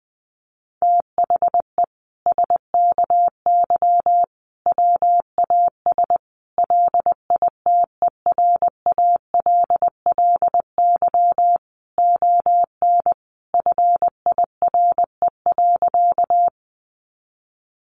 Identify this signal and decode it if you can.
Morse code, heard as a keyed tone, about 19 wpm, 700 Hz. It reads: THE SKY WAS LITERALLY ON FIRE.
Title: Sentences from Top 1400 Words 20wpm